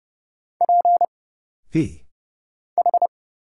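Morse code, heard as a keyed tone, about 30 wpm, 700 Hz. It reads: P H
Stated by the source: Morse Code Ninja single-character practice (letters, numbers, punctuation, prosigns) at 30 wpm, by Kurt Zoglmann (AD0WE).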